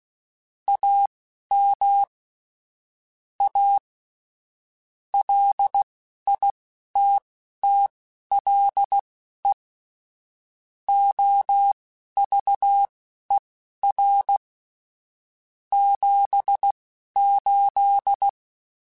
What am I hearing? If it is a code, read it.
Morse code, heard as a keyed tone, about 16 wpm, spaced as for 11 wpm, 800 Hz. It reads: AM A LITTLE OVER 78
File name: QSOs / QSOs-mp3s / kc7haa@10